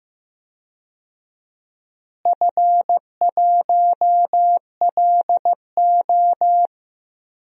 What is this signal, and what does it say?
Morse code, heard as a keyed tone, about 15 wpm, 700 Hz. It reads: F1LO
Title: DXCC-100 Call Signs - 4 Characters - Repeated 2x - 15wpm